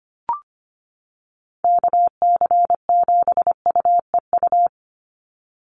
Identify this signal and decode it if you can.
Morse code, heard as a keyed tone, about 25 wpm, 700 Hz. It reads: KC7UEU
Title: USA Call Signs 25wpm